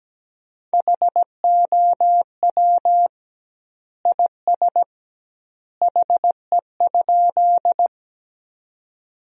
Morse code, heard as a keyed tone, about 17 wpm, 700 Hz. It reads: HOW IS HE?
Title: 2x Word-Spacing - Sentences from Top 100 Words - 17wpm